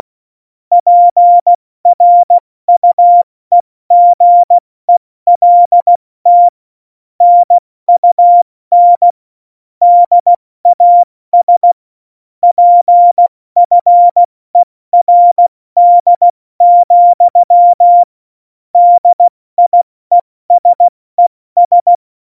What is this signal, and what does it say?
Morse code, heard as a keyed tone, about 16 wpm, 700 Hz. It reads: PRUEGELT NUN DAS PFERD, DIESES
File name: Morsekurs / Witze / Witz 21 Juli 2020, 70 F80